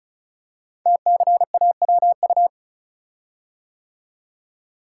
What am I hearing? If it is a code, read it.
Morse code, heard as a keyed tone, about 36 wpm, 700 Hz. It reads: TCAWU
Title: Code Groups - 5 Letter:Numbers - 35wpm